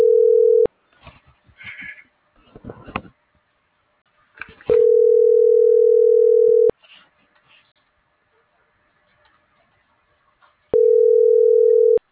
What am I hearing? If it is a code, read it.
Ringback tone (North America), 440 + 480 Hz.